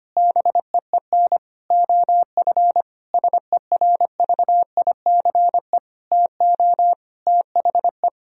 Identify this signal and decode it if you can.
Morse code, heard as a keyed tone, about 25 wpm, 700 Hz. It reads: BEEN OF SERVICE TO THE
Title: Sentences from Top 1000 Words - Review Part 4 - 25wpm